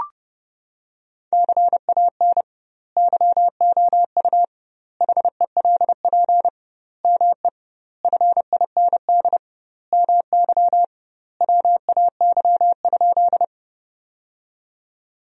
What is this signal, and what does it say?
Morse code, heard as a keyed tone, about 30 wpm, 700 Hz. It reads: CAN YOU HELP ME FIND MY WAY?
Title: 2x Word-Spacing - Sentences from Top 200 Words - 30wpm